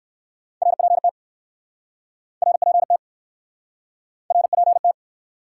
Morse code, heard as a keyed tone, about 62 wpm, 700 Hz. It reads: RPT RPT RPT